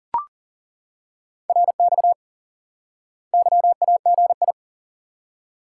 Morse code, heard as a keyed tone, about 40 wpm, 700 Hz. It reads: RX YAGI
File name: IQE Course 3 - Lesson 89 - YAGI - Two Elements - 40wpm